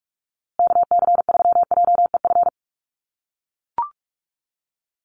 Morse code, heard as a keyed tone, about 44 wpm, 700 Hz. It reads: KC3JEF
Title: USA Call Signs Repeated 3x 45wpm